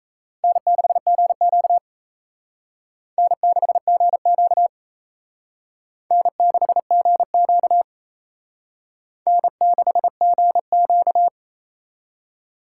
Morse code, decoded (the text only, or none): N6GQ N6GQ N6GQ N6GQ